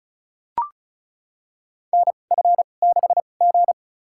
Morse code, heard as a keyed tone, about 34 wpm, spaced as for 25 wpm, 700 Hz. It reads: NF6G
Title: USA Call Signs - 4 Characters - 2x Character Spacing - 35wpm